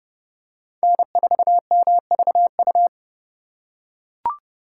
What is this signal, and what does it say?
Morse code, heard as a keyed tone, about 29 wpm, 700 Hz. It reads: N4MVU